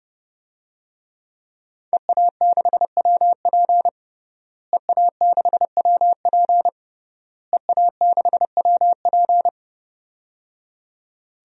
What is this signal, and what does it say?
Morse code, heard as a keyed tone, about 30 wpm, 700 Hz. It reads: EA6WP EA6WP EA6WP
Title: DXCC-100 Call Signs - 5 Characters - Repeated 3x - 30wpm